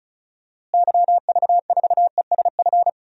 Morse code, heard as a keyed tone, about 34 wpm, 700 Hz. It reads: YV4ESF
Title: DXCC-100 Call Signs - 6 Characters - Repeated 3x - 35wpm